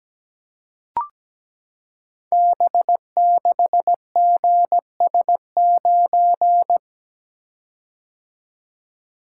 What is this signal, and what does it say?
Morse code, heard as a keyed tone, about 17 wpm, 700 Hz. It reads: B6GS9
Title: Code Groups - 5 Letter:Numbers - 17wpm